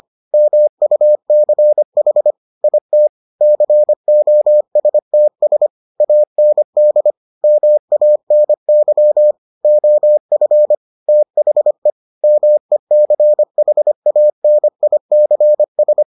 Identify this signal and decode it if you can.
Morse code, heard as a keyed tone, about 25 wpm, 600 Hz. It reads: MUCH IT COSTS AND MANY OF THE MECHANICS